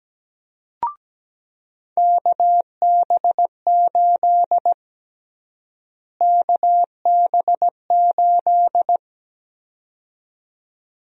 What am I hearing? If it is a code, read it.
Morse code, heard as a keyed tone, about 17 wpm, 700 Hz. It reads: KB8 KB8
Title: USA Call Sign Prefixes - Repeated 2x - 17wpm